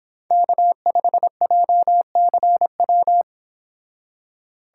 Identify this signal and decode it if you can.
Morse code, heard as a keyed tone, about 26 wpm, 700 Hz. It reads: K5JCW
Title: Speed Racing - USA 1x3 Call Signs 22wpm